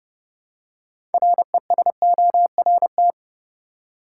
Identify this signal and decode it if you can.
Morse code, heard as a keyed tone, about 30 wpm, 700 Hz. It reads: RESORT
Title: Six Letter Words - 30wpm